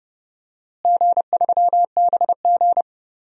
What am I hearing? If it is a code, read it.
Morse code, heard as a keyed tone, about 31 wpm, 700 Hz. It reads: G3BG
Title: DXCC-100 Call Signs - 4 Characters - Repeated 2x - 30wpm